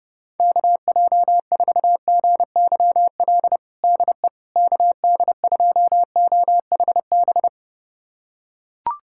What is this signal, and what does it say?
Morse code, heard as a keyed tone, about 31 wpm, 700 Hz. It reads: KJ4GYL DE KD2OHB
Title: USA Callsign DE Callsign - 6 Characters - 30wpm